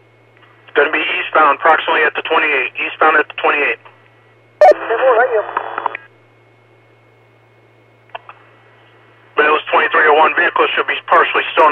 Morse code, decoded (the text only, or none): none